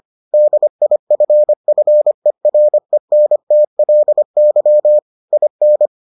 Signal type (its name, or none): Morse code